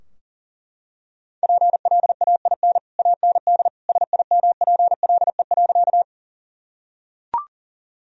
Morse code, heard as a keyed tone, about 40 wpm, 700 Hz. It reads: PLAIN AND SIMPLE.
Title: Sentences from Top 1000 Words - Review Part 2 - 40wpm